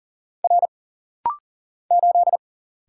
Morse code, heard as a keyed tone, about 38 wpm, 700 Hz. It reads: R 8
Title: Single Letter-Number Alt - Rapid-Fire 40wpm